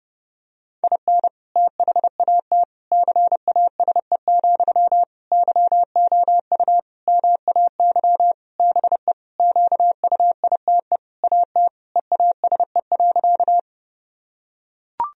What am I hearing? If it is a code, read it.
Morse code, heard as a keyed tone, about 30 wpm, 700 Hz. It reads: IN THAT CASE, YOU MAY BE QUITE AT EASE.
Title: Sentences from Top 1000 Words - Review Part 1 - 30wpm